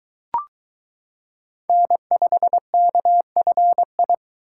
Morse code, heard as a keyed tone, about 22 wpm, 700 Hz. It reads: N5KFI